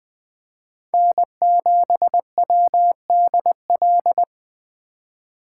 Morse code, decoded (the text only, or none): N7WDL